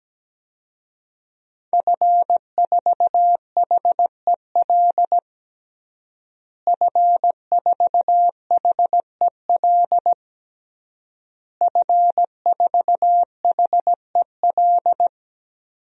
Morse code, heard as a keyed tone, about 17 wpm, 700 Hz. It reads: F4HEL F4HEL F4HEL
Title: DXCC-100 Call Signs - 5 Characters - Repeated 3x - 17wpm